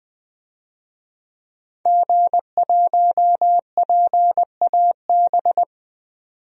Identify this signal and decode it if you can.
Morse code, heard as a keyed tone, about 20 wpm, 700 Hz. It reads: G1PAB
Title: DXCC-100 Call Signs - 5 Characters - 20wpm